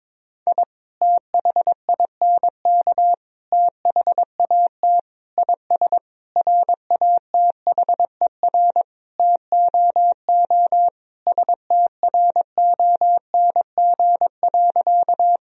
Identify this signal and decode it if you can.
Morse code, heard as a keyed tone, about 22 wpm, 700 Hz. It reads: I THINK THAT IS RATHER TOO STRONG.
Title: Sentences from Top 1000 Words - Review Part 4 - 22wpm